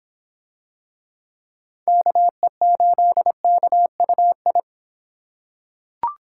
Morse code, heard as a keyed tone, about 24 wpm, 700 Hz. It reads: KE8KUI